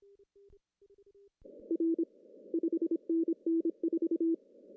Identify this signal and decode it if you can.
Morse code, heard as a keyed tone, about 26 wpm, 330 Hz. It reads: R 5NN 4
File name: CW / cqww-cw-k5tr